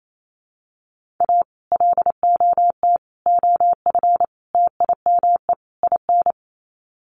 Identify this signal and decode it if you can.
Morse code, heard as a keyed tone, about 28 wpm, 700 Hz. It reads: A LOT OF TIME IN
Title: Sets of 5 Words Encore2 28wpm